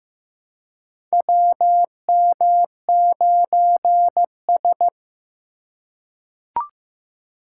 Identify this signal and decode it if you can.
Morse code, heard as a keyed tone, about 15 wpm, 700 Hz. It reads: WM9S
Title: USA Call Signs - 4 Characters - 2x Character Spacing - 15wpm